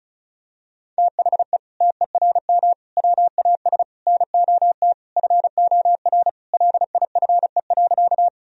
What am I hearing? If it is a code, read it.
Morse code, heard as a keyed tone, about 36 wpm, 700 Hz. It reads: THE TERM WAS NOT FOR LIFE.
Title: Sentences from Top 1000 Words - Review Part 4 - 35wpm